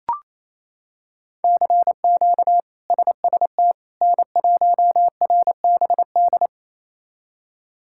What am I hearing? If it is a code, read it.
Morse code, heard as a keyed tone, about 28 wpm, 700 Hz. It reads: CQ SST N1RBD
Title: K1USN SST QSO Practice - 28wpm